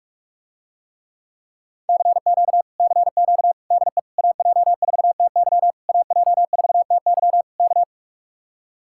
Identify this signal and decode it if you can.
Morse code, heard as a keyed tone, about 48 wpm, 700 Hz. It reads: CQ CQ DE AJ4TY AJ4TY K